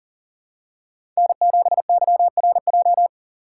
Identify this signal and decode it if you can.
Morse code, heard as a keyed tone, about 40 wpm, 700 Hz. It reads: N7YRJ
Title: USA Call Signs - 5 Characters - 6x Character Spacing - 40wpm